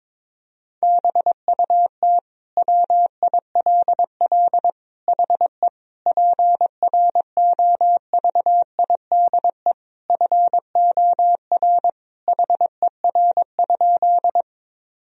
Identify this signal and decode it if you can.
Morse code, heard as a keyed tone, about 22 wpm, 700 Hz. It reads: BUT WILL HE PROVIDE FOR HER?
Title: Sentences from Top 1000 Words - Review Part 3 - 22wpm